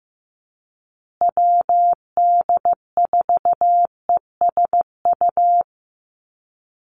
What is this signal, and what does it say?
Morse code, heard as a keyed tone, about 15 wpm, 700 Hz. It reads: WD4ESU
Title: USA Call Signs Repeated 3x 15wpm